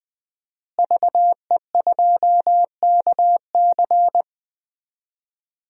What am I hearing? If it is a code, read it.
Morse code, heard as a keyed tone, about 20 wpm, 700 Hz. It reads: VE2KC
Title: DXCC-100 Call Signs - 5 Characters - Repeated 3x - 20wpm